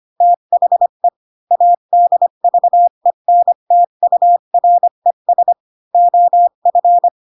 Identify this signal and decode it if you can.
Morse code, heard as a keyed tone, about 25 wpm, 700 Hz. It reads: THE ADVENTURES OF
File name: Pinocchio / Pinocchio ch5 20 25 ik1hge